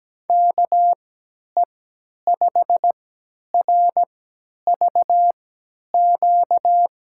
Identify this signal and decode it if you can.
Morse code, heard as a keyed tone, about 17 wpm, spaced as for 10 wpm, 700 Hz. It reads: KE5RVQ
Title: USA Call Signs - 6 Characters - 3x Character Spacing - 17wpm